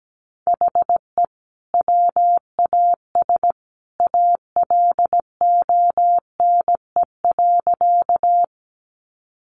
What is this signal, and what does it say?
Morse code, heard as a keyed tone, about 17 wpm, 700 Hz. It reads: HE WAS ALONE.
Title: Sentences from Top 1000 Words 17wpm